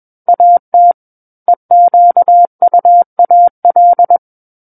Morse code, heard as a keyed tone, about 21 wpm, 700 Hz. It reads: AT EQUAL
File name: Book One / 12-Chapter 12 - 21 WPM